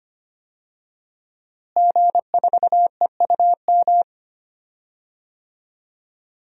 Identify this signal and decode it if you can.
Morse code, heard as a keyed tone, about 25 wpm, 700 Hz. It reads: G4EUM
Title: DXCC-100 Call Signs - 5 Characters - Repeated 3x - 25wpm